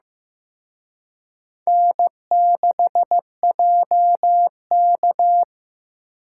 Morse code, heard as a keyed tone, about 18 wpm, 700 Hz. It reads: N6JK
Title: Speed Racing - USA 1x2 Call Signs 15wpm